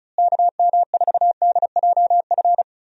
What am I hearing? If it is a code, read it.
Morse code, heard as a keyed tone, about 36 wpm, 700 Hz. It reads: KM4DJF
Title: USA Call Signs - 6 Characters - Repeated 2x - 35wpm